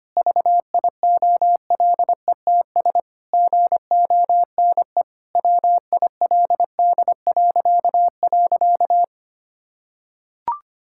Morse code, heard as a keyed tone, about 25 wpm, 700 Hz. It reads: VIOLETS GONE WILD..
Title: Pangram Challenge v2 - 25wpm